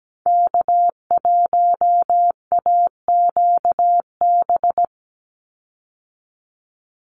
Morse code, heard as a keyed tone, about 17 wpm, 700 Hz. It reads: K1AQB
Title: Call Signs Repeated 2x - 17wpm